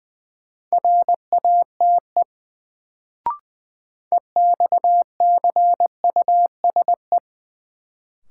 Morse code, heard as a keyed tone, about 20 wpm, 700 Hz. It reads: RATE EXCUSE
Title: Sentences from Top 700 Words - New Word Drill - 20wpm